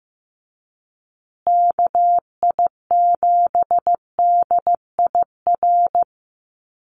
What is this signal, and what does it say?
Morse code, heard as a keyed tone, about 15 wpm, 700 Hz. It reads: KI7DIR
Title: USA Call Signs Repeated 3x 15wpm